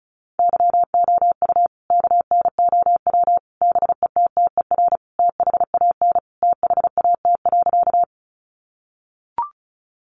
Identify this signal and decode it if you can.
Morse code, heard as a keyed tone, about 34 wpm, 700 Hz. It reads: YOU KNOW BETTER THAN THAT.
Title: Sentences from Top 400 Words 35wpm